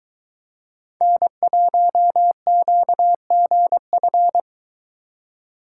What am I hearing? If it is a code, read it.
Morse code, heard as a keyed tone, about 22 wpm, 700 Hz. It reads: N1QGF